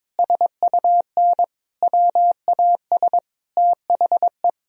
Morse code, heard as a keyed tone, about 21 wpm, 700 Hz. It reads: SUN WAS THE
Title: Sentences from Top 1000 Words - Review Part 2 - 22wpm